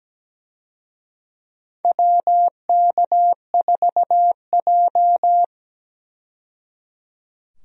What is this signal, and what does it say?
Morse code, heard as a keyed tone, about 17 wpm, 700 Hz. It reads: WK4J
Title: USA Call Signs - 4 Characters - 17wpm